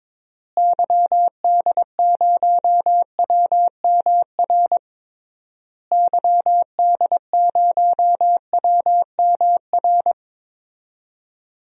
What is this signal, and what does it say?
Morse code, heard as a keyed tone, about 22 wpm, 700 Hz. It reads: YD0WMR YD0WMR
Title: DXCC-100 Call Signs - 6 Characters - Repeated 3x - 22wpm